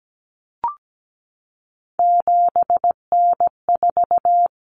Morse code, heard as a keyed tone, about 17 wpm, 700 Hz. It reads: 7N4
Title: Common Cut Numbers - 3-Digit 17wpm